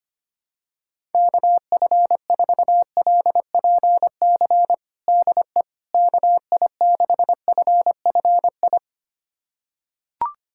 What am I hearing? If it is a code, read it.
Morse code, heard as a keyed tone, about 25 wpm, 700 Hz. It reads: KF4LPC DE KI6FFI
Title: USA Callsign DE Callsign - 6 Characters - 25wpm